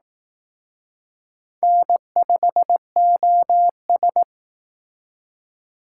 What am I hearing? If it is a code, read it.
Morse code, heard as a keyed tone, about 18 wpm, 700 Hz. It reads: N5OS
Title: Speed Racing - USA 1x2 Call Signs 15wpm